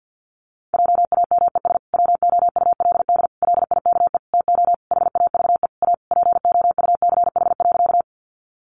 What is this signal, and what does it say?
Morse code, heard as a keyed tone, about 50 wpm, 700 Hz. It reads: JAMES WOULD LIKE TO HAVE A POUCH.
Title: Easy Phrases 50wpm